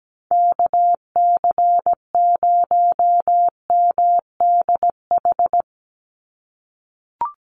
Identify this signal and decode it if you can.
Morse code, heard as a keyed tone, about 17 wpm, 700 Hz. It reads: KC0MDH
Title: USA Call Signs 17wpm